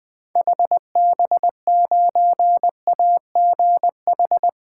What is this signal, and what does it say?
Morse code, heard as a keyed tone, about 20 wpm, 700 Hz. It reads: HB9AGH
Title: 2023 SOTA Chaser Call Signs - 20wpm